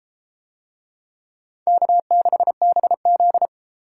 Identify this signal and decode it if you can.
Morse code, heard as a keyed tone, about 30 wpm, 700 Hz. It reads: K6BZ